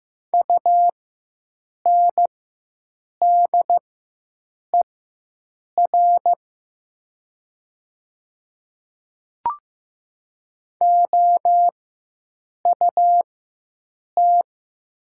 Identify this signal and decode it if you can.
Morse code, heard as a keyed tone, about 15 wpm, spaced as for 7 wpm, 700 Hz. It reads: UNDER OUT
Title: 4x Character Spacing - Top 200 Words - 15wpm